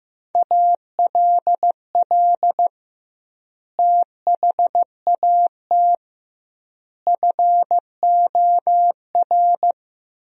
Morse code, heard as a keyed tone, about 16 wpm, 700 Hz. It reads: ALL THAT FOR
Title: 2x Word-Spacing - Sentences from Top 500 Words - 15wpm